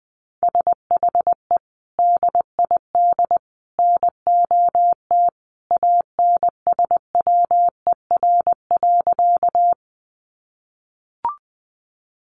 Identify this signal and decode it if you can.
Morse code, heard as a keyed tone, about 20 wpm, 700 Hz. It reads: SHE DID NOT ANSWER.
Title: Sentences from Top 600 Words 20wpm